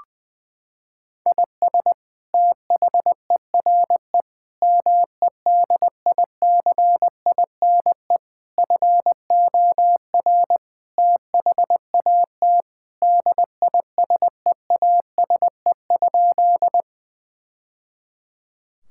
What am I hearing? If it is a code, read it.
Morse code, heard as a keyed tone, about 20 wpm, 700 Hz. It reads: IS THERE MEDICINE FOR THAT DISEASE?
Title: Sentences from Top 1000 Words - Review Part 6 - 20wpm